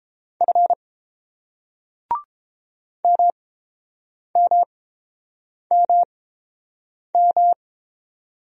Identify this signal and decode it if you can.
Morse code, heard as a keyed tone, about 31 wpm, spaced as for 7 wpm, 700 Hz. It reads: F MMMM